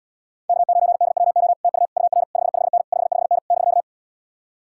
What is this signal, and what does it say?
Morse code, heard as a keyed tone, about 77 wpm, 700 Hz. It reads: K9DWO TU UR 55N 55N <BK>